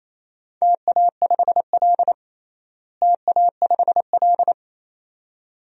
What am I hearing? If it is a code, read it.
Morse code, heard as a keyed tone, about 28 wpm, 700 Hz. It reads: TA5L TA5L
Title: DXCC-100 Call Signs - 4 Characters - Repeated 2x - 28wpm